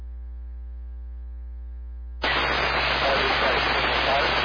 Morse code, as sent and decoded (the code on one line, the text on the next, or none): none